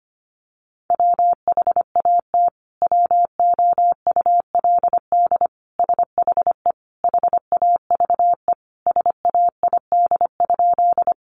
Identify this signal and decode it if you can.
Morse code, heard as a keyed tone, about 25 wpm, 700 Hz. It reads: WHAT WOULD SHE HAVE SAID?
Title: Sentences from Top 100 Words 25wpm